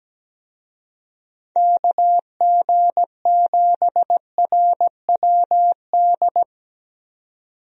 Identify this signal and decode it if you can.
Morse code, heard as a keyed tone, about 17 wpm, 700 Hz. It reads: KG7RWD
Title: USA Call Signs - 6 Characters - 17wpm